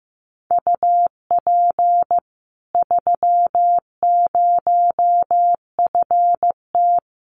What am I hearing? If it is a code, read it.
Morse code, heard as a keyed tone, about 15 wpm, 700 Hz. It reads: UP 30FT
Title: Common QSO Phrases 15wpm